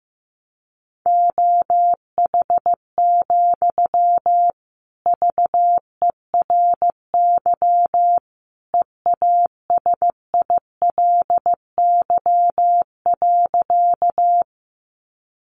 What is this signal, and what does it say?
Morse code, heard as a keyed tone, about 15 wpm, 700 Hz. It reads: OH, VERY EASILY.
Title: Sentences from Top 900 Words 15wpm